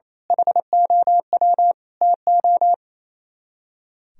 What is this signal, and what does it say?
Morse code, heard as a keyed tone, about 35 wpm, 700 Hz. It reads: HOW TO
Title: Speed-Racing - Sets of 5 Words 28wpm